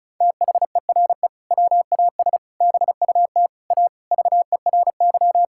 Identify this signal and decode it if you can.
Morse code, heard as a keyed tone, about 34 wpm, 700 Hz. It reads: THERE WAS BUT A VERY
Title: Sentences from Top 1000 Words - Review Part 5 - 35wpm